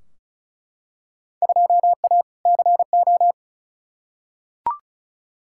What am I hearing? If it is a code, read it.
Morse code, heard as a keyed tone, about 34 wpm, 700 Hz. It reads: 2A CO
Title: Field Day Exchanges - 35wpm